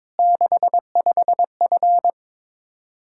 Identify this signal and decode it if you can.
Morse code, heard as a keyed tone, about 22 wpm, 700 Hz. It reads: 65F
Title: Practice QSO - Exchange 2 - Info - 22wpm